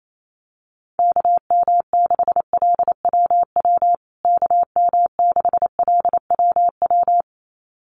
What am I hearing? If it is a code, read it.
Morse code, heard as a keyed tone, about 28 wpm, 700 Hz. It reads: KM6LWW KM6LWW
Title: USA Call Signs Repeated 2x 28wpm